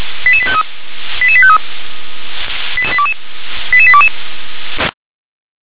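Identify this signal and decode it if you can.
Five-tone selective call (ZVEI), heard as quick runs of five tones, tones of about 70 ms. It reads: 88853 88853 88822
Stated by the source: Five-tone ZVEI selective call received on an SDR.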